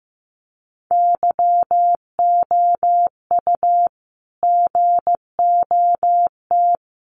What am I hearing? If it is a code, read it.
Morse code, heard as a keyed tone, about 15 wpm, 700 Hz. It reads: YOU GOT
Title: Sentences from Top 300 Words 15wpm